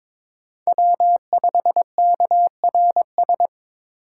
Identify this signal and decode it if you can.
Morse code, heard as a keyed tone, about 21 wpm, 700 Hz. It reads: W5KRS
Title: USA Call Signs - 4x Character Spacing - 22wpm